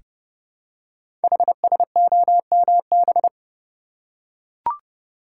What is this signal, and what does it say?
Morse code, heard as a keyed tone, about 30 wpm, 700 Hz. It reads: HSOMB
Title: Code Groups - 5 Letters - 30wpm